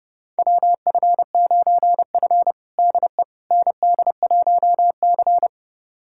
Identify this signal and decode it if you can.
Morse code, heard as a keyed tone, about 32 wpm, 700 Hz. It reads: WF9F DE ND1C
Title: USA Callsign DE Callsign - 4 Characters - 30wpm